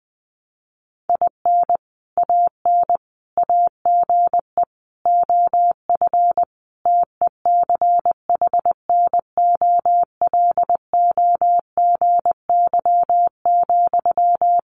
Morse code, heard as a keyed tone, about 19 wpm, 700 Hz. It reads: IN AN AGE OF TECHNOLOGY,
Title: Sentences from Top 1200 Words 20wpm